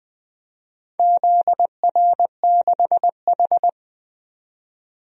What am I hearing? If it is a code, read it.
Morse code, heard as a keyed tone, about 20 wpm, 700 Hz. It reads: ZR6H
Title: DXCC-100 Call Signs - 4 Characters - 20wpm